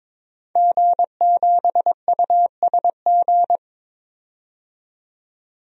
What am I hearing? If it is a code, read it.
Morse code, heard as a keyed tone, about 22 wpm, 700 Hz. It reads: G7USG
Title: DXCC-100 Call Signs - 5 Characters - 22wpm